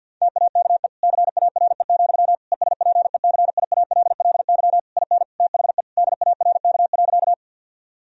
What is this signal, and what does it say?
Morse code, heard as a keyed tone, about 50 wpm, 700 Hz. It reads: TAKE CARE, ESPECIALLY IN THE DARK.
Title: Sentences from Top 1000 Words - Review Part 4 - 50wpm